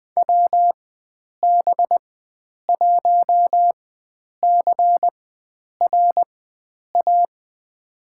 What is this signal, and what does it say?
Morse code, heard as a keyed tone, about 20 wpm, spaced as for 9 wpm, 700 Hz. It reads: WB1CRA
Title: USA Call Signs - 6 Characters - 4x Character Spacing - 20wpm